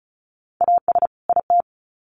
Morse code, heard as a keyed tone, about 34 wpm, 700 Hz. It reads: AS IT
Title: Sets of 2 Words 35wpm